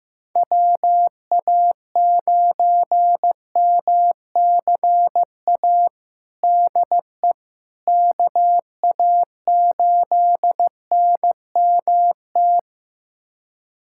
Morse code, heard as a keyed tone, about 15 wpm, 700 Hz. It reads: WA9MCA DE KA8NMT
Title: USA Callsign DE Callsign - 6 Characters - 15wpm